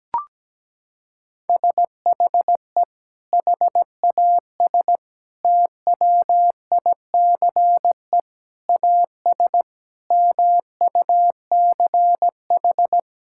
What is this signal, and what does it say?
Morse code, heard as a keyed tone, about 17 wpm, 700 Hz. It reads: SHE HAS TWICE AS MUCH
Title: Sentences from Top 1600 Words 17wpm